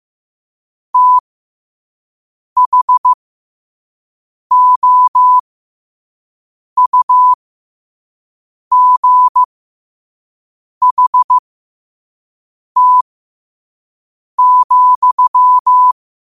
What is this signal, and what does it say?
Morse code, heard as a keyed tone, about 15 wpm, spaced as for 5 wpm, 1000 Hz. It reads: THOUGHT,
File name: qotdfiles / 5wpm